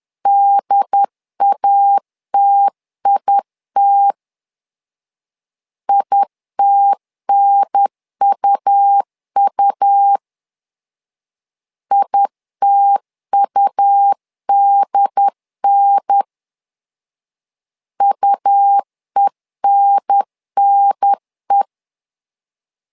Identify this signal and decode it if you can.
Morse code, heard as a keyed tone, about 10 wpm, 800 Hz. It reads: DATIT ITNUU ITUDN UENNE
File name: LEZIONI / 2 UHDN / 2 UHDN 10wpm 1904